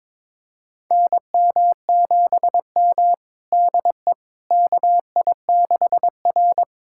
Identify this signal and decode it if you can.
Morse code, heard as a keyed tone, about 21 wpm, 700 Hz. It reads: NM7M DE KI6R
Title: USA Callsign DE Callsign - 4 Characters - 22wpm